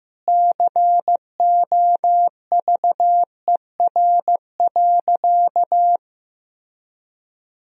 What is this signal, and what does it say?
Morse code, heard as a keyed tone, about 15 wpm, 700 Hz. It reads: COVER.
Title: Sentences from Top 1000 Words - Review Part 1 - 15wpm